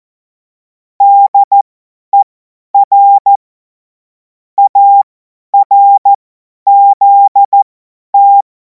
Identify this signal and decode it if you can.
Morse code, heard as a keyed tone, about 14 wpm, spaced as for 10 wpm, 800 Hz. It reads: DER ARZT
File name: Morsekurs / Witze / Witz 18. August 2026 50 F70